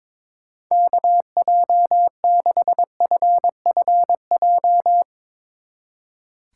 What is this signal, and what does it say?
Morse code, heard as a keyed tone, about 22 wpm, 700 Hz. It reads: KJ6FFJ
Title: Speed Racing - USA 2x3 Call Signs 22wpm